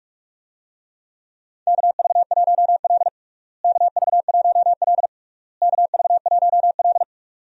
Morse code, heard as a keyed tone, about 46 wpm, 700 Hz. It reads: KV1L KV1L KV1L
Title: USA Call Signs - 4 Characters - Repeated 3x - 45wpm